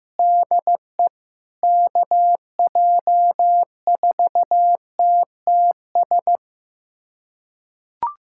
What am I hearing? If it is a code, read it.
Morse code, heard as a keyed tone, about 15 wpm, 700 Hz. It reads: DE KJ4TTS
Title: USA Callsign DE Callsign - 6 Characters - 15wpm